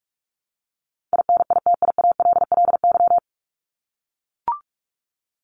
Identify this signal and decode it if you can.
Morse code, heard as a keyed tone, about 44 wpm, 700 Hz. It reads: INITIALLY
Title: Top 1251-1500 Words 45wpm